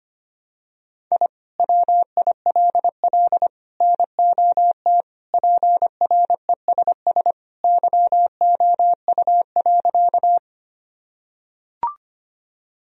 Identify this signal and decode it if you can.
Morse code, heard as a keyed tone, about 25 wpm, 700 Hz. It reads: I WILL NOT PRESS YOU.
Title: Sentences from Top 1000 Words - Review Part 1 - 25wpm